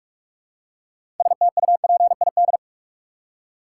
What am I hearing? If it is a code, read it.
Morse code, heard as a keyed tone, about 46 wpm, 700 Hz. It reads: STUPID
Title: Six Letter Words - 45wpm